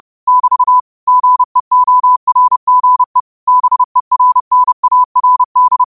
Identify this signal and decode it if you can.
Morse code, heard as a keyed tone, about 30 wpm, 1000 Hz. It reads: <BT> GEORGE BERNARD
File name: qotdfiles / 30wpm